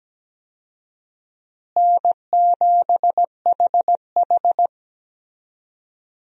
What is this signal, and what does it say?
Morse code, heard as a keyed tone, about 17 wpm, 700 Hz. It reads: N7HH
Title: USA Call Signs - 4 Characters - Repeated 2x - 17wpm